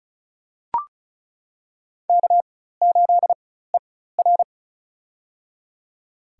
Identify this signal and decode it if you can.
Morse code, heard as a keyed tone, about 36 wpm, spaced as for 16 wpm, 700 Hz. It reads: K8ER
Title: USA Call Signs - 4 Characters - 4x Character Spacing - 35wpm